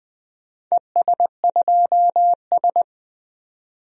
Morse code, heard as a keyed tone, about 20 wpm, 700 Hz. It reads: ES2S
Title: DXCC-100 Call Signs - 4 Characters - 20wpm